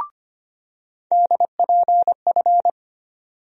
Morse code, heard as a keyed tone, about 25 wpm, 700 Hz. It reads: DPF
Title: USA Call Sign Postfixes - 3 Letters 25wpm